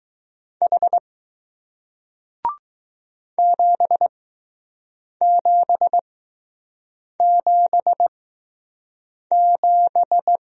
Morse code, heard as a keyed tone, about 20 wpm, spaced as for 6 wpm, 700 Hz. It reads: H 7777